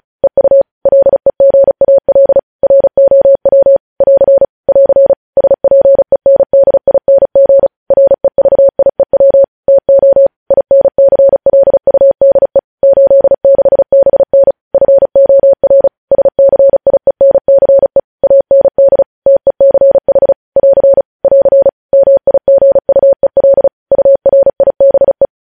Morse code, heard as a keyed tone, about 36 wpm, 550 Hz. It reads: EU LEGAL ROW <AR> <AR> SPENDING REVIEW TO INCLUDE 86BN FOR SCIENCE AND TECH <AR> <AR> MIGUEL URIBE
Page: news headlines in morse code at 35 wpm